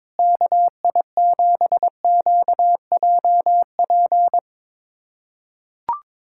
Morse code, heard as a keyed tone, about 22 wpm, 700 Hz. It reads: KI7QJP
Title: USA Callsign DE Callsign - 6 Characters - 22wpm